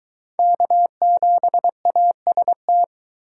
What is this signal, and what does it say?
Morse code, heard as a keyed tone, about 23 wpm, 700 Hz. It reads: K7AST